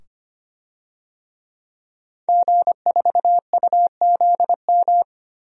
Morse code, heard as a keyed tone, about 25 wpm, 700 Hz. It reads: G4UZM
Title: DXCC-100 Call Signs - 5 Characters - 25wpm